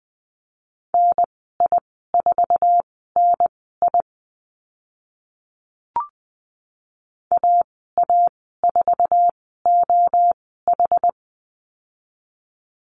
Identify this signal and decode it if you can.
Morse code, heard as a keyed tone, about 20 wpm, spaced as for 15 wpm, 700 Hz. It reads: NI4NI AA4OH
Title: 076 Lesson 13g - Letter L - Call Signs 20wpm